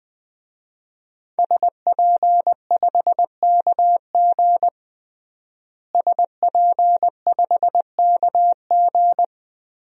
Morse code, heard as a keyed tone, about 21 wpm, 700 Hz. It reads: SP5KG SP5KG
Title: DXCC-100 Call Signs - 5 Characters - Repeated 2x - 20wpm